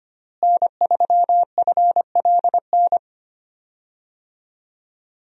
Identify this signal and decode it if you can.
Morse code, heard as a keyed tone, about 25 wpm, 700 Hz. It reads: N3FLN
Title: USA Call Signs - 5 Characters - Repeated 2x - 25wpm